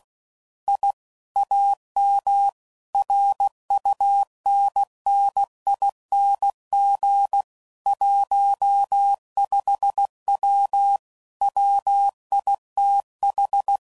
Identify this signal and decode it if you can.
Morse code, heard as a keyed tone, about 16 wpm, 800 Hz. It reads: I AM RUNNING 15W WITH
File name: CW Code Groups @15WPM / QSOs / WD0BGP Spaced